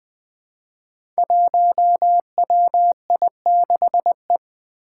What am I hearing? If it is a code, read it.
Morse code, heard as a keyed tone, about 20 wpm, 700 Hz. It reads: 1WI6E
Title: Five Challenging Letter-Number Combinations 20wpm